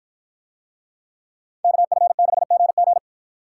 Morse code, heard as a keyed tone, about 51 wpm, 700 Hz. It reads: KF6GG